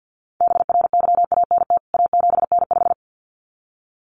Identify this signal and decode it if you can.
Morse code, heard as a keyed tone, about 50 wpm, 700 Hz. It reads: BRYANT A7N5
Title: CWT Exchange 50wpm